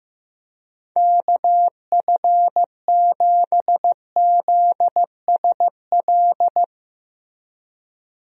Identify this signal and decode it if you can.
Morse code, heard as a keyed tone, about 16 wpm, 700 Hz. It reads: KF7ZSL
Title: USA Call Signs - 6 Characters - Repeated 2x - 15wpm